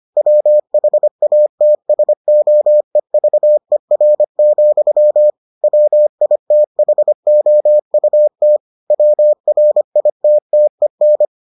Morse code, heard as a keyed tone, about 25 wpm, 600 Hz. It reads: WHATSOEVER, WITHOUT WRITTEN